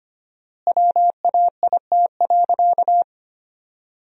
Morse code, heard as a keyed tone, about 25 wpm, 700 Hz. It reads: WAIT.
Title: 4x Word-Spacing - Sentences from Top 500 Words - 25wpm